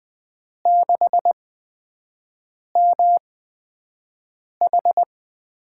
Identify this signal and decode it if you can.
Morse code, heard as a keyed tone, about 20 wpm, spaced as for 5 wpm, 700 Hz. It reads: 6MH
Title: USA Call Signs - 8x Character Spacing - 20wpm